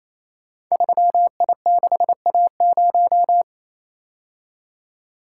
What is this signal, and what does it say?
Morse code, heard as a keyed tone, about 27 wpm, 700 Hz. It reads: 3I6A0
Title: Code Groups - 5 Letter:Numbers - 28wpm